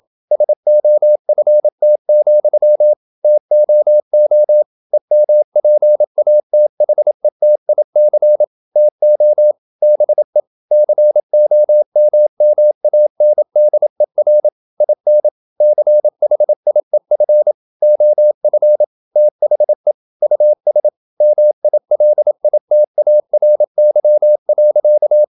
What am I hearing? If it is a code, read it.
Morse code, heard as a keyed tone, about 27 wpm, 600 Hz. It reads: SOFT, TOO EMPATHETIC TO BE COMMANDER IN CHIEF OF THE US MILITARY.
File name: Never 0027wpm